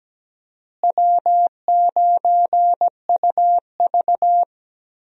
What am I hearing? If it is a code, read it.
Morse code, heard as a keyed tone, about 17 wpm, 700 Hz. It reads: W9UV
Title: USA Call Signs - 4 Characters - Repeated 2x - 17wpm